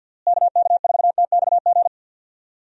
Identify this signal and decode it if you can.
Morse code, heard as a keyed tone, about 51 wpm, 700 Hz. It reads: KK4TXG
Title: USA Call Signs - 6 Characters - Repeated 2x - 50wpm